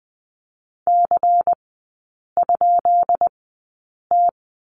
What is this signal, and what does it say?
Morse code, heard as a keyed tone, about 19 wpm, spaced as for 8 wpm, 700 Hz. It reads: C?T
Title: 133 Lesson 22e - Question Mark - Three Characters 20wpm